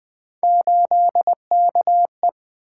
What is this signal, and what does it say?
Morse code, heard as a keyed tone, about 20 wpm, 700 Hz. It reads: 8KE
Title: Three Letter-Number Combinations - 20wpm